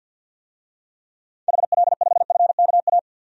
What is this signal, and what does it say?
Morse code, heard as a keyed tone, about 52 wpm, 700 Hz. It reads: HL5FKA
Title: DXCC-100 Call Signs - 6 Characters - 50wpm